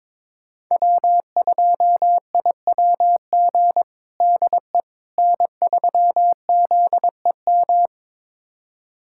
Morse code, heard as a keyed tone, about 22 wpm, 700 Hz. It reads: W2IWG DE N3ZEM
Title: USA Callsign DE Callsign - 5 Characters - 22wpm